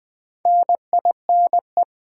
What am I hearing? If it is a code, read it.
Morse code, heard as a keyed tone, about 20 wpm, 700 Hz. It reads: NINE
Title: Build A Word - 2 Letters At A Time - 6-Letter Words - 20wpm